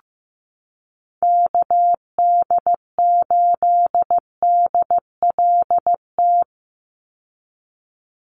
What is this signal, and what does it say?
Morse code, heard as a keyed tone, about 15 wpm, 700 Hz. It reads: KD8DLT
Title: USA Call Signs 15wpm